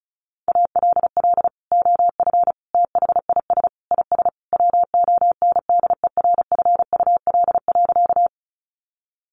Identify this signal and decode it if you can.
Morse code, heard as a keyed tone, about 34 wpm, 700 Hz. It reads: ALL OF THIS IS WONDERFUL.
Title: Sentences from Top 1100 Words 35wpm